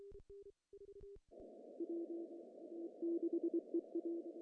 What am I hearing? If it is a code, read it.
Morse code, heard as a keyed tone, about 23 wpm, spaced as for 16 wpm, 340 Hz. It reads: W6EA
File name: CW / cqww-cw-k5tr